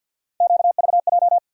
Morse code, heard as a keyed tone, about 47 wpm, 690 Hz. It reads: QVJ
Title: USA Call Sign Postfixes - 3 Letters 50wpm